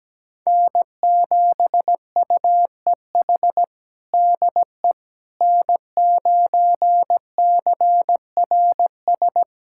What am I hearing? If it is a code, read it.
Morse code, heard as a keyed tone, about 17 wpm, 700 Hz. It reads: N7UEH DE N9CRS
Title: USA Callsign DE Callsign - 5 Characters - 17wpm